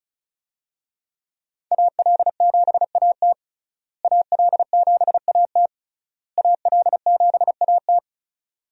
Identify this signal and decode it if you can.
Morse code, heard as a keyed tone, about 36 wpm, 700 Hz. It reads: AL7AT AL7AT AL7AT
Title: USA Call Signs - 5 Characters - Repeated 3x - 35wpm